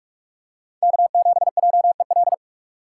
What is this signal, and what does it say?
Morse code, heard as a keyed tone, about 45 wpm, 690 Hz. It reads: K7JEL